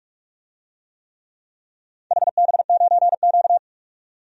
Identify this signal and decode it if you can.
Morse code, heard as a keyed tone, about 44 wpm, 700 Hz. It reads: HB9Q
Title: DXCC-100 Call Signs - 4 Characters - 45wpm